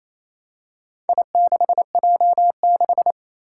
Morse code, heard as a keyed tone, about 27 wpm, 700 Hz. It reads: I6J6
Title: Four Challenging Letter-Number Combinations 28wpm